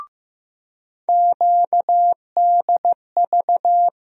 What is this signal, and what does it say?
Morse code, heard as a keyed tone, about 15 wpm, 700 Hz. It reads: QDV
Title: USA Call Sign Postfixes - 3 Letters 15wpm